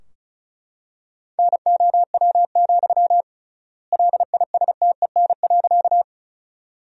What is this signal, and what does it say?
Morse code, heard as a keyed tone, about 34 wpm, 700 Hz. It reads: NOW, LISTEN.
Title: 3x Word-Spacing - Sentences from Top 400 Words - 35wpm